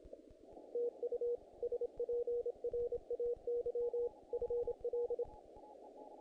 Morse code, heard as a keyed tone, about 27 wpm, 480 Hz. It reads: TU SPRAY FL
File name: contests / naqp-cw-k5tr-op-k5pi- -1800